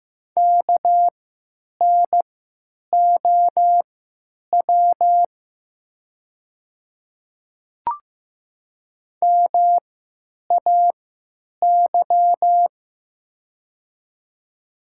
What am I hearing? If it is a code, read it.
Morse code, heard as a keyed tone, about 15 wpm, spaced as for 9 wpm, 700 Hz. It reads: KNOW MAY
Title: 3x Character Spacing - Top 200 Words - 15wpm